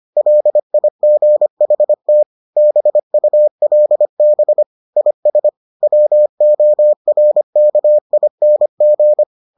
Morse code, heard as a keyed tone, about 25 wpm, 600 Hz. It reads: LIGHT BULB IS WORKING